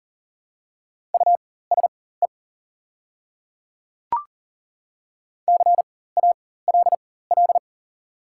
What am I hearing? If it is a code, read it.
Morse code, heard as a keyed tone, about 40 wpm, spaced as for 19 wpm, 690 Hz. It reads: USE CALL
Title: 4x Character Spacing - Top 100 Words - 40wpm